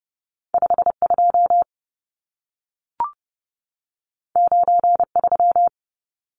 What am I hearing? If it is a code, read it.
Morse code, heard as a keyed tone, about 30 wpm, 700 Hz. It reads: 52 93
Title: Numbers - 2-Digit 30wpm